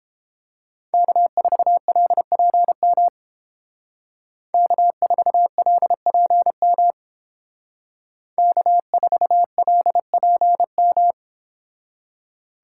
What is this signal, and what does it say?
Morse code, heard as a keyed tone, about 28 wpm, 700 Hz. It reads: K4LPM K4LPM K4LPM